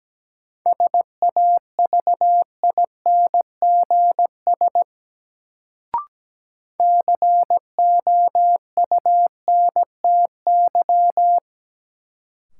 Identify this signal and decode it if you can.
Morse code, heard as a keyed tone, about 17 wpm, 700 Hz. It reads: SAVINGS COUNTY
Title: Sentences from Top 1200 Words - New Word Drill Alt - 17wpm